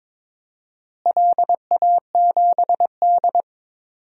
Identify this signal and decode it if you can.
Morse code, heard as a keyed tone, about 22 wpm, 700 Hz. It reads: LA7D
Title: DXCC-100 Call Signs - 4 Characters - Repeated 2x - 22wpm